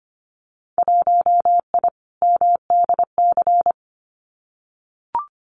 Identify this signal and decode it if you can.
Morse code, heard as a keyed tone, about 25 wpm, 700 Hz. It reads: 1I MDC
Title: Winter Field Day Exchanges 25wpm